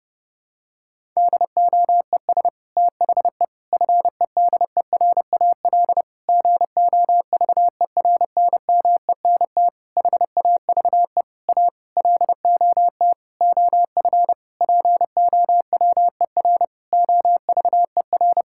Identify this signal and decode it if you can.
Morse code, heard as a keyed tone, about 32 wpm, 700 Hz. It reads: DOES THE FEDERAL GOVERNMENT HAVE A LOT OF POWER OVER
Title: Sentences from Top 1000 Words - Review Part 5 - 30wpm